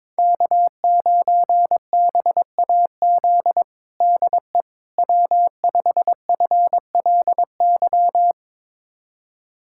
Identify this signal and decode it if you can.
Morse code, heard as a keyed tone, about 22 wpm, 700 Hz. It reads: K9BAZ DE W5FLY
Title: USA Callsign DE Callsign - 5 Characters - 22wpm